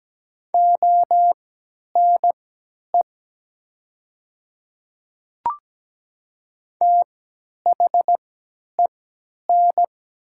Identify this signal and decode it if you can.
Morse code, heard as a keyed tone, about 17 wpm, spaced as for 10 wpm, 700 Hz. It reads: ONE THEN
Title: 3x Character Spacing - Top 100 Words - 17wpm